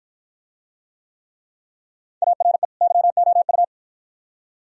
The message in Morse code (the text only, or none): ARE YOU